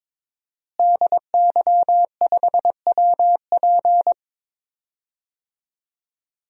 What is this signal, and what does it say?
Morse code, heard as a keyed tone, about 22 wpm, 700 Hz. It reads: DY5WP
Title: DXCC-100 Call Signs - 5 Characters - Repeated 3x - 22wpm